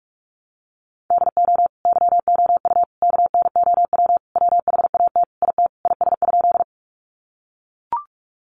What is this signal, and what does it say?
Morse code, heard as a keyed tone, about 44 wpm, 700 Hz. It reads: DO YOU KNOW WHAT IT IS?
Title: Sentences from Top 100 Words 45wpm